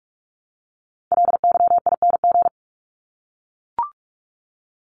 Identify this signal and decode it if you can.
Morse code, heard as a keyed tone, about 44 wpm, 700 Hz. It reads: LYING
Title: ING Suffix - Top 100 Words with Base Words 45wpm